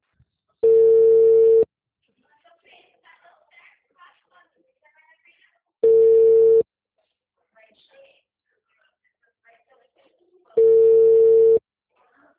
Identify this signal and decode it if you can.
Ringback tone (China, 450 Hz), with speech.